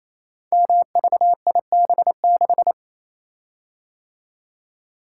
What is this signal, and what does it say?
Morse code, heard as a keyed tone, about 27 wpm, 700 Hz. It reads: MVIB6
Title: Code Groups - 5 Letter:Numbers - 28wpm